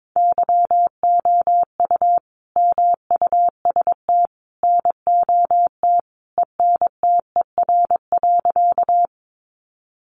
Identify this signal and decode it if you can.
Morse code, heard as a keyed tone, about 22 wpm, 700 Hz. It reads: YOU MUST NOT ENTER.
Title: Sentences from Top 1300 Words 22wpm